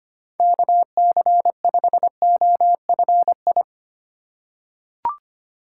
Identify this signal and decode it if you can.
Morse code, heard as a keyed tone, about 25 wpm, 700 Hz. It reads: KC5OFI
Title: USA Call Signs - 6 Characters - 2x Character Spacing - 25wpm